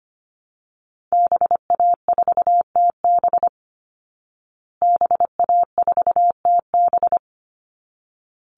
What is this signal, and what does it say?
Morse code, heard as a keyed tone, about 25 wpm, 700 Hz. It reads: BA4TB BA4TB
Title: Call Signs Repeated 3x - 25wpm